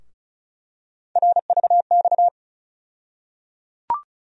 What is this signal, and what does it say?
Morse code, heard as a keyed tone, about 36 wpm, 700 Hz. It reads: RVX
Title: USA Call Sign Postfixes - 3 Letters 35wpm